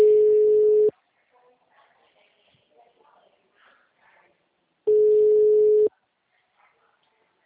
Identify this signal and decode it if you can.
Ringback tone (Europe, 425 Hz), with speech.
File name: out- -2004- - - .696